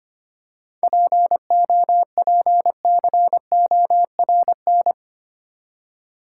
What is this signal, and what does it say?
Morse code, heard as a keyed tone, about 25 wpm, 690 Hz. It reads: POPCORN
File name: practice / 25wpm / Easy Words with Hints - 25wpm